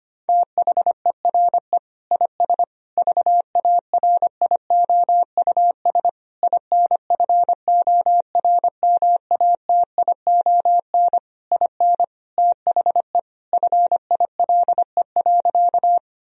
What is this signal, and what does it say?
Morse code, heard as a keyed tone, about 25 wpm, 700 Hz. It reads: THERE IS VARIOUS INFORMATION IN THE FILE.
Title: Sentences from Top 1000 Words - Review Part 4 - 25wpm